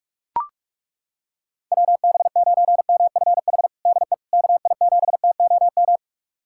Morse code, heard as a keyed tone, about 41 wpm, 700 Hz. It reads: WB9MFH DE KI7TOM
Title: USA Callsign DE Callsign - 6 Characters - 45wpm